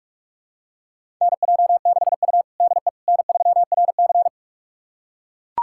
Morse code, heard as a keyed tone, about 41 wpm, 700 Hz. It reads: NJ6U DE N3RC
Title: USA Callsign DE Callsign - 4 Characters - 45wpm